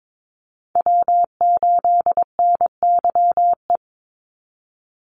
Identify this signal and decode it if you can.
Morse code, heard as a keyed tone, about 22 wpm, 700 Hz. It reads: W8NYE
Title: USA Call Signs 22wpm